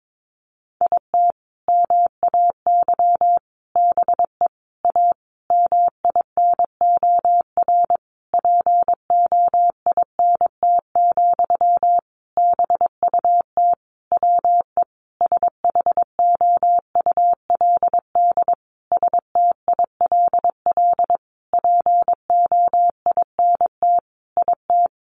Morse code, heard as a keyed tone, about 22 wpm, 700 Hz. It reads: IT MAY BE A MINOR POINT, BUT WE SHOULD STILL POINT IT
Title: Sentences from Top 1500 Words 22wpm